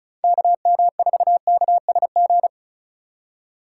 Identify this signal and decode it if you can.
Morse code, heard as a keyed tone, about 35 wpm, 690 Hz. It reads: KM4KSG
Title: USA Call Signs - 6 Characters - 6x Character Spacing - 35wpm